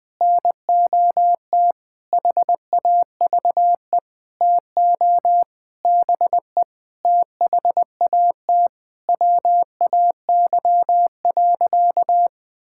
Morse code, heard as a keyed tone, about 20 wpm, 700 Hz. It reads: NOT HAVE TO BE THAT WAY.
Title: Sentences with 8-Word Phrases - 20wpm